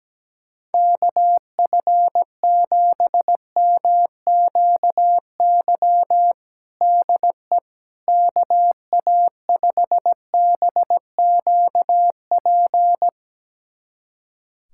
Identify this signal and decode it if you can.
Morse code, heard as a keyed tone, about 17 wpm, 700 Hz. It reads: KF7MQY DE KA5BQP
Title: USA Callsign DE Callsign - 6 Characters - 17wpm